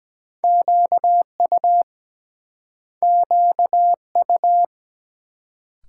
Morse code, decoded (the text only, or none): QU QU